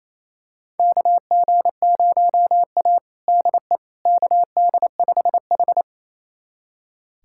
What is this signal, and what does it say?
Morse code, heard as a keyed tone, about 27 wpm, 700 Hz. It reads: KG0A DE KD5H
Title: USA Callsign DE Callsign - 4 Characters - 28wpm